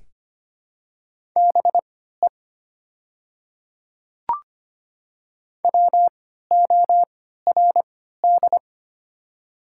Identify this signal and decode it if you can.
Morse code, heard as a keyed tone, about 25 wpm, spaced as for 14 wpm, 700 Hz. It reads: BE WORD
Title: 3x Character Spacing - Top 300 Words - 25wpm